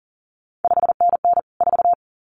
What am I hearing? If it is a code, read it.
Morse code, heard as a keyed tone, about 42 wpm, 700 Hz. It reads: 5NN 4
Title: CQ WW DX Contest Exchange Encore - 40wpm